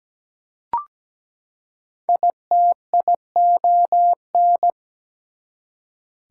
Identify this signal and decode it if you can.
Morse code, heard as a keyed tone, about 17 wpm, 700 Hz. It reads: ITION
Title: Prefixes and Suffixes 17wpm